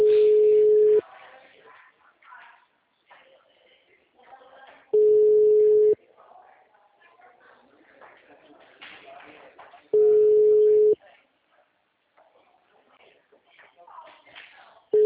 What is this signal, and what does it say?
Ringback tone (Europe, 425 Hz), with speech.